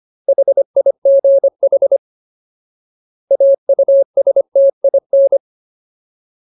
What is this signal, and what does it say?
Morse code, heard as a keyed tone, about 25 wpm, 550 Hz. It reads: HIGH AUSTIN
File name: Ultimate Word List / 25WPM Ultimate